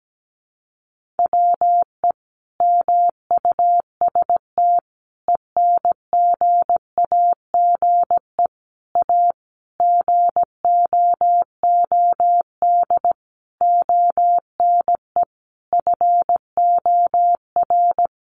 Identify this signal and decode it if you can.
Morse code, heard as a keyed tone, about 17 wpm, 700 Hz. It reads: WE MUST ENGAGE A GOOD ONE FOR
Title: Sentences from Top 1500 Words 17wpm